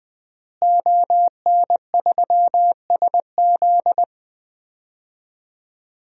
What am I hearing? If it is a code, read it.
Morse code, heard as a keyed tone, about 20 wpm, 700 Hz. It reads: ON3SZ
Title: DXCC-100 Call Signs - 5 Characters - Repeated 3x - 20wpm